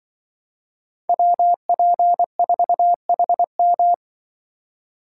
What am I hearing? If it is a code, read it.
Morse code, heard as a keyed tone, about 24 wpm, 700 Hz. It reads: WP4HM
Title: Speed Racing - USA 2x2 Call Signs 20wpm